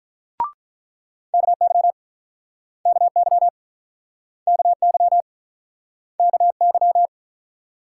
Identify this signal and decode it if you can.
Morse code, heard as a keyed tone, about 48 wpm, 700 Hz. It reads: KY KY KY KY